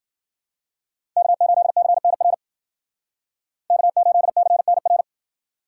Morse code, heard as a keyed tone, about 52 wpm, 700 Hz. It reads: K8CNR K8CNR